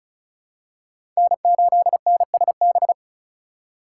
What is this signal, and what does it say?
Morse code, heard as a keyed tone, about 36 wpm, 700 Hz. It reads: N8NSB
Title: USA Call Signs - 5 Characters - 3x Character Spacing - 35wpm